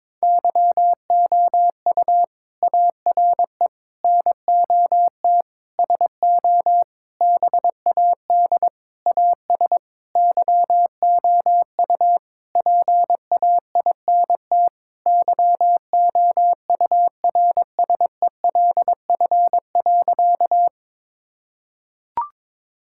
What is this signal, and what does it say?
Morse code, heard as a keyed tone, about 22 wpm, 700 Hz. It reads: YOU ARE NOT SO BAD AS YOU PAINT YOURSELF.
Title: Sentences from Top 1000 Words - Review Part 6 - 22wpm